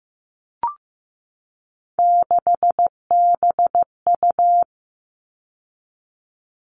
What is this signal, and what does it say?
Morse code, heard as a keyed tone, about 15 wpm, 700 Hz. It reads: 6BU
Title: All Cut Numbers - 3-Digit 15wpm